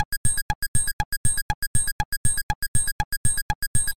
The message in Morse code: none